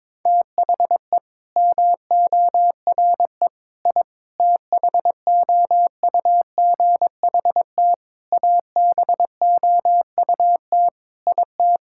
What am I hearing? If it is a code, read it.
Morse code, heard as a keyed tone, about 22 wpm, 700 Hz. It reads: THE MORE I THOUGHT ABOUT IT
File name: practice / 22wpm / Sets of 8 Words - 22wpm